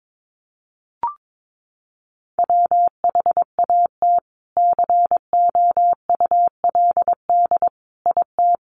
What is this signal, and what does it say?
Morse code, heard as a keyed tone, about 22 wpm, 700 Hz. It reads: WHAT COULD IT
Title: Sentences from Top 100 Words 22wpm